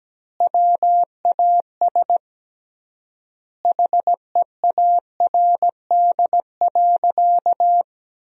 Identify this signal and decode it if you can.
Morse code, heard as a keyed tone, about 17 wpm, 700 Hz. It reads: WAS HEARD.
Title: 3x Word-Spacing - Sentences from Top 400 Words - 17wpm